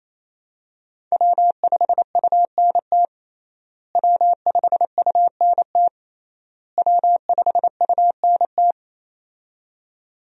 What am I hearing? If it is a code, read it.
Morse code, heard as a keyed tone, about 28 wpm, 700 Hz. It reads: W5UNT W5UNT W5UNT
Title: USA Call Signs - 5 Characters - Repeated 3x - 28wpm